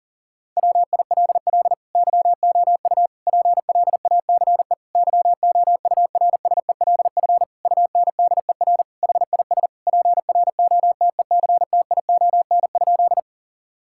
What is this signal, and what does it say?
Morse code, heard as a keyed tone, about 37 wpm, 700 Hz. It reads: WILL YOU PLACE YOURSELF UNDER HIS PROTECTION?
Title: Sentences from Top 1600 Words 40wpm